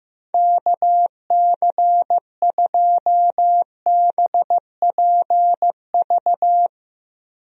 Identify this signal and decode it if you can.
Morse code, heard as a keyed tone, about 15 wpm, 700 Hz. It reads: KC2BPV
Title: USA Call Signs - 6 Characters - Repeated 2x - 15wpm